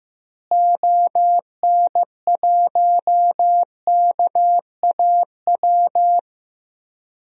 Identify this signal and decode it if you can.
Morse code, heard as a keyed tone, about 15 wpm, 700 Hz. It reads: ON1KAW
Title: DXCC-100 Call Signs - 6 Characters - Repeated 3x - 15wpm